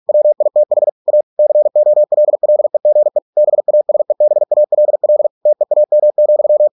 Morse code, heard as a keyed tone, about 46 wpm, 600 Hz. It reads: WITH A COLLEGE BASEBALL TEAM,